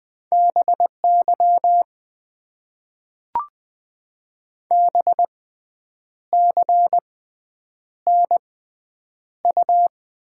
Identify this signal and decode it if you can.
Morse code, heard as a keyed tone, about 20 wpm, spaced as for 8 wpm, 700 Hz. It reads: BY BCNU
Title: Top 90 QSO Elements - 6x Character Spacing - 20wpm